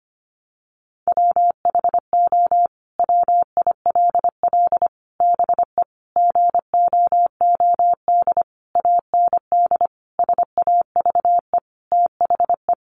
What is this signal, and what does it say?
Morse code, heard as a keyed tone, about 25 wpm, 700 Hz. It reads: WHO WILL BE GOOD AND SAVE THE
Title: Sentences from Top 1100 Words 25wpm